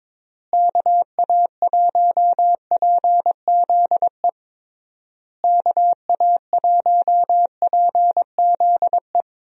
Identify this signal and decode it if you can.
Morse code, heard as a keyed tone, about 22 wpm, 700 Hz. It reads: KA1PZE KA1PZE
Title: USA Call Signs - 6 Characters - Repeated 3x - 22wpm